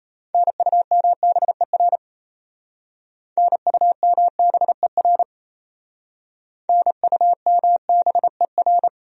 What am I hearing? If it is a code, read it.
Morse code, heard as a keyed tone, about 37 wpm, 700 Hz. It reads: NUMBER NUMBER NUMBER